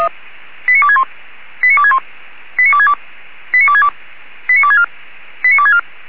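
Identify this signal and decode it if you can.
Five-tone selective call (ZVEI), heard as quick runs of five tones, tones of about 70 ms. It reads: 89261 89261 89262 89262 89264 89264
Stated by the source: ZVEI five-tone selective call received on SDR.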